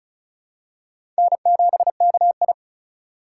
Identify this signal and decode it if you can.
Morse code, heard as a keyed tone, about 36 wpm, 700 Hz. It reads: N7KI
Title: USA Call Signs - 4 Characters - 3x Character Spacing - 35wpm